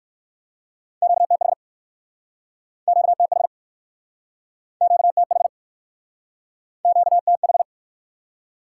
Morse code, heard as a keyed tone, about 58 wpm, 700 Hz. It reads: QTH QTH QTH QTH